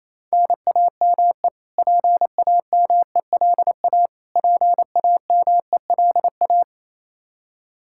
Morse code, heard as a keyed tone, about 28 wpm, 700 Hz. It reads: NAME PAMELA PAMELA
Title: Practice QSO - Exchange 1 - Beginning - 28wpm